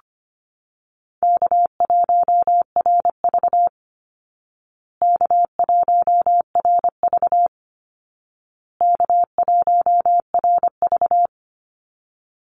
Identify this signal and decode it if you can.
Morse code, heard as a keyed tone, about 25 wpm, 700 Hz. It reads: K1RV K1RV K1RV
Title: Easy Call Signs Repeated 3x - 25wpm